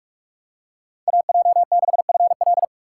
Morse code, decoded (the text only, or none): AJ6FL